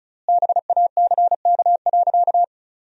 Morse code, heard as a keyed tone, about 36 wpm, 700 Hz. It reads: BACK.
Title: 4x Word-Spacing - Sentences from Top 300 Words - 35wpm